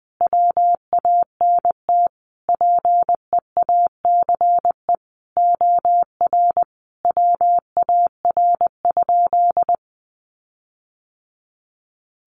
Morse code, heard as a keyed tone, about 19 wpm, 700 Hz. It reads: WANT PEACE OR WAR?
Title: Sentences from Top 1500 Words 20wpm